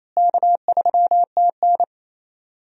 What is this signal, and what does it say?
Morse code, heard as a keyed tone, about 28 wpm, 700 Hz. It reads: K3TN
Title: USA Call Signs - 4 Characters - Repeated 3x - 28wpm